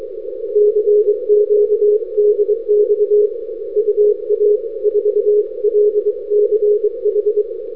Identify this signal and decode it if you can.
Morse code, heard as a keyed tone, about 23 wpm, 420 Hz. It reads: CQ DX UA4LCH
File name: n6tr / ua4lch-9-Feb-2008